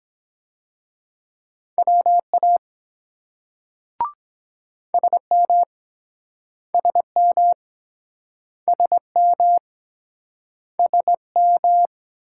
Morse code, heard as a keyed tone, about 23 wpm, 700 Hz. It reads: WA SM SM SM SM